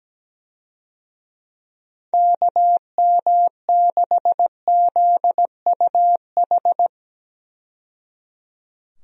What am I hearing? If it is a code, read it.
Morse code, heard as a keyed tone, about 17 wpm, 700 Hz. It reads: KM6ZUH
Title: USA Call Signs - 6 Characters - 17wpm